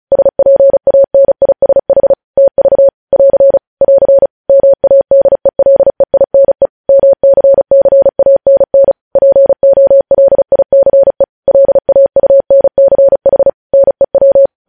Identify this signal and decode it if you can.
Morse code, heard as a keyed tone, about 36 wpm, 550 Hz. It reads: SPANISH TV <AR> <AR> MADELEINE MCCANN POLICE LAUNCH NEW
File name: podcast / News headlines-1x-35x35